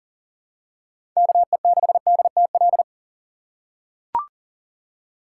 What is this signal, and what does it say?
Morse code, heard as a keyed tone, about 37 wpm, 690 Hz. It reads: KE6DTL
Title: USA Call Signs - 8x Character Spacing - 40wpm